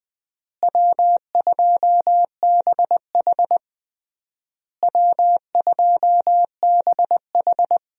Morse code, heard as a keyed tone, about 21 wpm, 700 Hz. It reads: W2BH W2BH
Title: USA Call Signs - 4 Characters - Repeated 3x - 20wpm